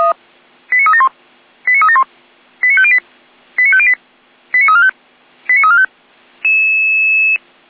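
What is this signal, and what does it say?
Five-tone selective call (ZVEI), heard as quick runs of five tones, tones of about 80 ms. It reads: 89371 89371 89508 89508 89345 89345
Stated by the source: Received on a software-defined radio (SDR); ZVEI five-tone selective call 89340.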